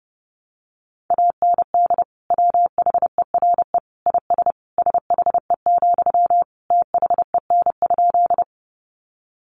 Morse code, heard as a keyed tone, about 30 wpm, 700 Hz. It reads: AND WHERE IS SHE, THEN?
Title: Sentences from Top 200 Words 30wpm